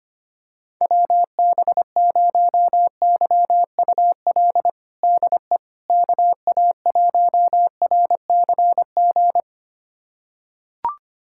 Morse code, heard as a keyed tone, about 25 wpm, 700 Hz. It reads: WB0YUL DE KA1RCG
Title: USA Callsign DE Callsign 25wpm